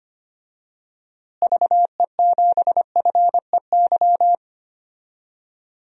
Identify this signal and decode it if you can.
Morse code, heard as a keyed tone, about 25 wpm, 700 Hz. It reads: VE7FEY
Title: DXCC-100 Call Signs - 6 Characters - Repeated 3x - 25wpm